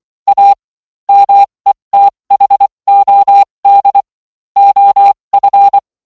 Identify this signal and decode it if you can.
Morse code, heard as a keyed tone, about 24 wpm, spaced as for 20 wpm, 800 Hz. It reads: A METHOD OF